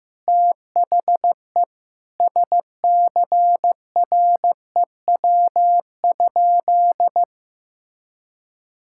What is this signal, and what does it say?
Morse code, heard as a keyed tone, about 16 wpm, 700 Hz. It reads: THE SCREW?
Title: Sentences from Top 1600 Words 15wpm